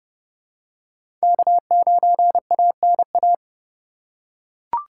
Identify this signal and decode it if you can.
Morse code, heard as a keyed tone, about 30 wpm, 700 Hz. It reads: K9ANA
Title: USA Call Signs - 5 Characters - 8x Character Spacing - 30wpm